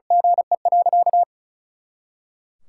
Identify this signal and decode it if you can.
Morse code, heard as a keyed tone, about 36 wpm, 700 Hz. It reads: GE.
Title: Pangram Challenge v2 - 35wpm